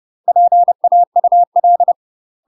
Morse code, heard as a keyed tone, about 30 wpm, 700 Hz. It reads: PAUL
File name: UK MASTER QSOs TRAINING / 30WPM / UK MASTER QSOs 0027